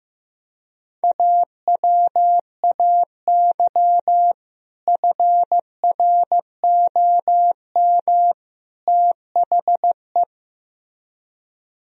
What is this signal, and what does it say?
Morse code, heard as a keyed tone, about 15 wpm, 700 Hz. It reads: AWAY FROM THE
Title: Speed-Racing - Sets of 3 Words 15wpm